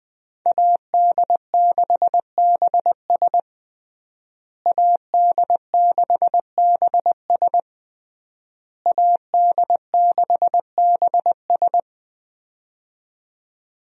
Morse code, heard as a keyed tone, about 21 wpm, 700 Hz. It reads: AD6BS AD6BS AD6BS
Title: USA Call Signs - 5 Characters - Repeated 3x - 20wpm